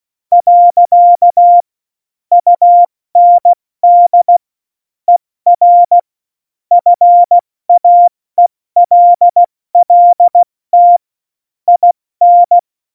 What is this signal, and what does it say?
Morse code, heard as a keyed tone, about 16 wpm, 700 Hz. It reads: . UND ER FAELLT IN
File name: Morsekurs / Witze / Witz 09. Oktober 2018, 70 F80